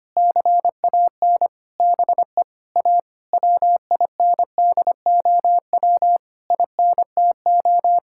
Morse code, heard as a keyed tone, about 25 wpm, 700 Hz. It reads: CAN BE A WINDOW INTO
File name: practice / 25wpm / Difficult Sentences with Hints - 25wpm